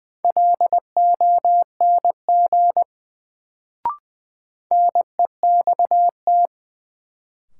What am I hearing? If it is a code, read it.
Morse code, heard as a keyed tone, about 20 wpm, 700 Hz. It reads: LONG NEXT
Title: Sets of 2 Words - New Word Drill - 20wpm